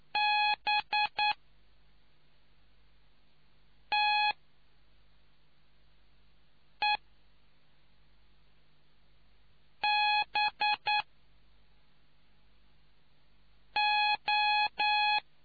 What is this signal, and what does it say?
Morse code, heard as a keyed tone, about 9 wpm, spaced as for 3 wpm, 840 Hz. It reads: BTEBO